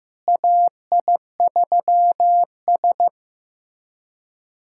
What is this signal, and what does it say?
Morse code, heard as a keyed tone, about 15 wpm, 700 Hz. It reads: AI3S
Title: USA Callsign DE Callsign - 4 Characters - 15wpm